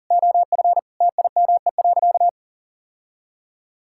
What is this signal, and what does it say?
Morse code, heard as a keyed tone, about 40 wpm, 700 Hz. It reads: OF TIME.
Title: Sentences from Top 1000 Words - Review Part 5 - 40wpm